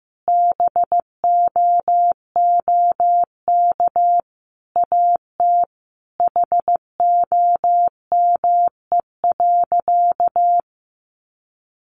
Morse code, heard as a keyed tone, about 15 wpm, 700 Hz. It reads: BOOK AT HOME.
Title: Sentences from Top 1400 Words 15wpm